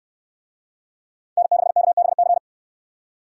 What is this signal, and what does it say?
Morse code, heard as a keyed tone, about 68 wpm, 700 Hz. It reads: N6KBC